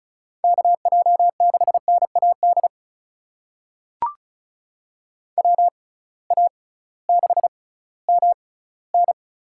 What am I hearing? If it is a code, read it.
Morse code, heard as a keyed tone, about 34 wpm, spaced as for 19 wpm, 700 Hz. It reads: KJ6NAD WA6MN
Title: USA Call Signs - 6 Characters - 6x Character Spacing - 35wpm